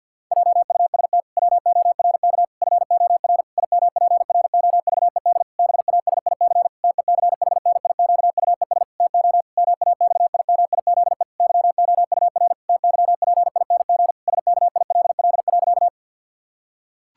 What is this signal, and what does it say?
Morse code, heard as a keyed tone, about 48 wpm, 700 Hz. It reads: JUST WORK FOR IMPROVED BASIC TECHNIQUES TO MAXIMIZE YOUR TYPING SKILL.
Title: Pangram Challenge v2 - 50wpm